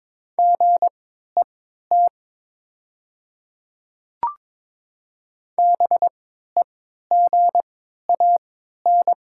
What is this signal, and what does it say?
Morse code, heard as a keyed tone, about 22 wpm, spaced as for 13 wpm, 700 Hz. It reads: GET BEGAN
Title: 3x Character Spacing - Top 500 Words - 22wpm